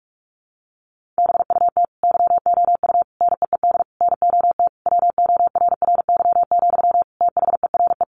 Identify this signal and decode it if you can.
Morse code, heard as a keyed tone, about 44 wpm, 700 Hz. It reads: BUT YOU NEED NOT WORRY, THERE
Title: Sentences from Top 1300 Words 45wpm